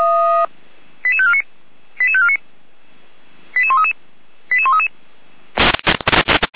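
Five-tone selective call (ZVEI), heard as quick runs of five tones, tones of about 70 ms. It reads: 88549 88549 88133 88133
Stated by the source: ZVEI five-tone selective call received on SDR.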